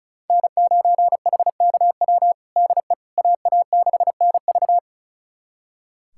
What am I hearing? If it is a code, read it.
Morse code, heard as a keyed tone, about 34 wpm, 700 Hz. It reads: N9HKW DE AA6NV
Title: USA Callsign DE Callsign - 5 Characters - 35wpm